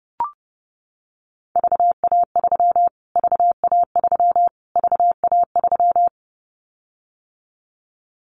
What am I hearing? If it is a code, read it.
Morse code, heard as a keyed tone, about 30 wpm, 700 Hz. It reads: VA3 VA3 VA3
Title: Callsign Prefixes for Top 25 DXCC - Repeated 3x - 30wpm